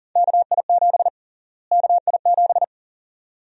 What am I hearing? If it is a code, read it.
Morse code, heard as a keyed tone, about 40 wpm, 700 Hz. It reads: KI7 KI7
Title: USA Call Sign Prefixes - Repeated 2x - 40wpm